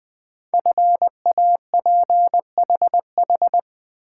Morse code, heard as a keyed tone, about 20 wpm, 700 Hz. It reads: FAPHH
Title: Code Groups - 5 Letters - 20wpm